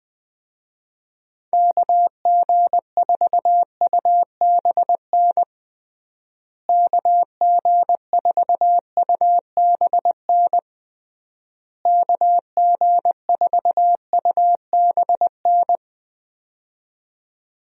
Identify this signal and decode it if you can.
Morse code, heard as a keyed tone, about 21 wpm, 700 Hz. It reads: KG4UBN KG4UBN KG4UBN
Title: USA Call Signs - 6 Characters - Repeated 3x - 20wpm